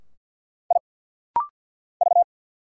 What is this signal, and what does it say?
Morse code, heard as a keyed tone, about 49 wpm, 700 Hz. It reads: I V
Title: Single Letters - Rapid-Fire 50wpm